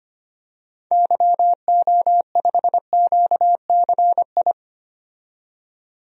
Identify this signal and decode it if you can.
Morse code, heard as a keyed tone, about 25 wpm, 700 Hz. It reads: YO5QCI
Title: DXCC-100 Call Signs - 6 Characters - 25wpm